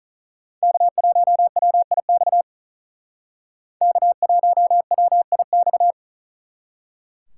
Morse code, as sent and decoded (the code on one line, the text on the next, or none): -.- .---- .-- .. -..- / -.- .---- .-- .. -..-
K1WIX K1WIX